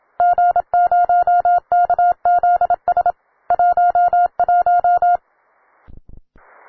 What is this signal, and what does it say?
Morse code, heard as a keyed tone, about 27 wpm, 710 Hz. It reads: G0KZS 11
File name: wae / cw beginner / ua9cdv cw